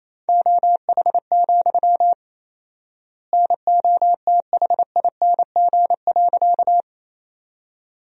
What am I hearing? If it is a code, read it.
Morse code, heard as a keyed tone, about 28 wpm, 700 Hz. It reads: OH, NOTHING.
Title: 4x Word-Spacing - Sentences from Top 500 Words - 28wpm